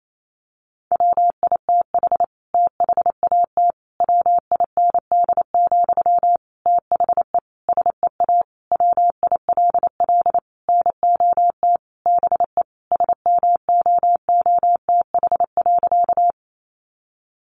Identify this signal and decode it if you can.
Morse code, heard as a keyed tone, about 28 wpm, 700 Hz. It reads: WITH THAT WIND, THE SEA WILL NOT BE SMOOTH.
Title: Sentences from Top 1400 Words 28wpm